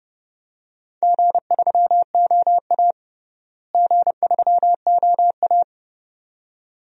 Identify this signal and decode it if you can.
Morse code, heard as a keyed tone, about 30 wpm, 700 Hz. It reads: G3OA G3OA
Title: DXCC-100 Call Signs - 4 Characters - Repeated 2x - 30wpm